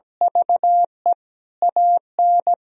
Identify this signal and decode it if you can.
Morse code, heard as a keyed tone, about 17 wpm, 700 Hz. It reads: VE AN
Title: Sentences from Top 1000 Words - Review Part 3 - 17wpm